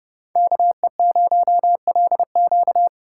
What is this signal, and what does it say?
Morse code, heard as a keyed tone, about 31 wpm, 700 Hz. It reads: KE0LQ
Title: USA Call Signs - 5 Characters - 2x Character Spacing - 30wpm